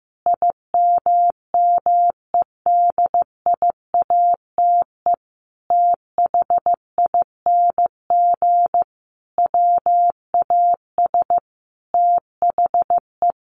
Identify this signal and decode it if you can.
Morse code, heard as a keyed tone, about 15 wpm, 700 Hz. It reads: IMMEDIATE THING WAS THE
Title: A Sound Of Thunder 15wpm